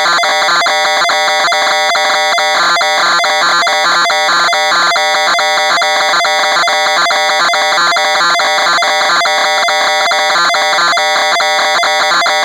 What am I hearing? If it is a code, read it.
Two-tone teleprinter signal, about 420 Hz shift.